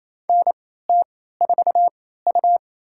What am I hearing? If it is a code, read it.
Morse code, heard as a keyed tone, about 28 wpm, spaced as for 16 wpm, 700 Hz. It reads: NT4U
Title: USA Call Signs - 4 Characters - 3x Character Spacing - 28wpm